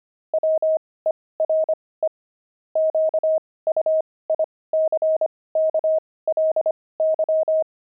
Morse code, heard as a keyed tone, about 25 wpm, spaced as for 18 wpm, 630 Hz. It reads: WERE QUICKLY
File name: pangram-fs-18-cs-25-sixty